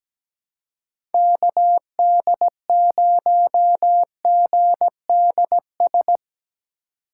Morse code, heard as a keyed tone, about 17 wpm, 700 Hz. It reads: KD0GDS
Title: USA Call Signs - 2x Character Spacing - 17wpm